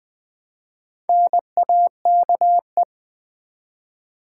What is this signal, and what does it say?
Morse code, heard as a keyed tone, about 20 wpm, 700 Hz. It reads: NAKE
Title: Build A Word - 1 Letter At A Time - 5-Letter Words - 20wpm